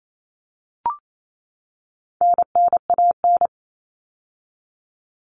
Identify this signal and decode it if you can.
Morse code, heard as a keyed tone, about 28 wpm, 700 Hz. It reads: NNAN
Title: All Cut Numbers - 4-Digit 28wpm